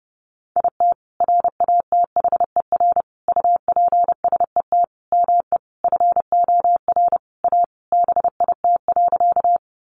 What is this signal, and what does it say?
Morse code, heard as a keyed tone, about 30 wpm, 700 Hz. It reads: IT RATHER UPSET ME FOR A BIT.
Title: Sentences from Top 900 Words 30wpm